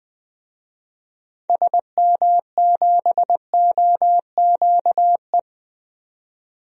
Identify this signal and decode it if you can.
Morse code, heard as a keyed tone, about 20 wpm, 700 Hz. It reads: SM7OQE
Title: DXCC-100 Call Signs - 6 Characters - 20wpm